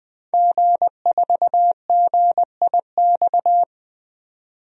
Morse code, heard as a keyed tone, about 20 wpm, 700 Hz. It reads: G4GIX
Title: DXCC-100 Call Signs - 5 Characters - 20wpm